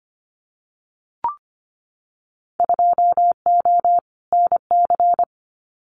Morse code, heard as a keyed tone, about 25 wpm, 700 Hz. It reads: 2O NC
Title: Winter Field Day Exchanges 25wpm